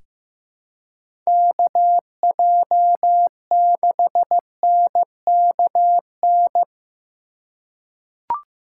Morse code, heard as a keyed tone, about 15 wpm, 700 Hz. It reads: KJ6NKN
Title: USA Call Signs - 2x Character Spacing - 15wpm